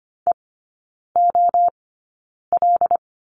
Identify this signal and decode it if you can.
Morse code, heard as a keyed tone, about 25 wpm, spaced as for 9 wpm, 700 Hz. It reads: EOL
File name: Lesson 14e - Number 2 - Three Characters 25wpm